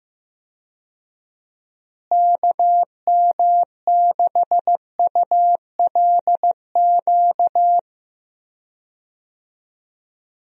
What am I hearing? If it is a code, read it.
Morse code, heard as a keyed tone, about 15 wpm, 700 Hz. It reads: KM6ULQ
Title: USA Call Signs - 6 Characters - Repeated 3x - 15wpm